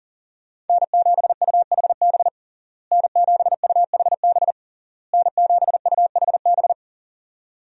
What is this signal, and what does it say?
Morse code, heard as a keyed tone, about 40 wpm, 700 Hz. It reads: N7UHB N7UHB N7UHB
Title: USA Call Signs - 5 Characters - Repeated 3x - 40wpm